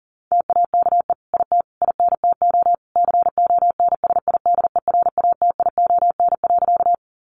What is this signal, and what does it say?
Morse code, heard as a keyed tone, about 40 wpm, 700 Hz. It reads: TAKE IT INTO CONSIDERATION.
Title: Sentences from Top 1500 Words 40wpm